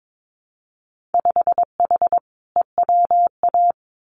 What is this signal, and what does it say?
Morse code, heard as a keyed tone, about 22 wpm, 700 Hz. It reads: 5H EWA
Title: Winter Field Day Exchanges Encore 22wpm